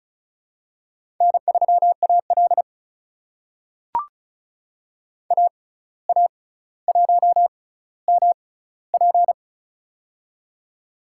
Morse code, heard as a keyed tone, about 36 wpm, spaced as for 17 wpm, 700 Hz. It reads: N3AL AA1MP
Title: USA Call Signs - 6x Character Spacing - 35wpm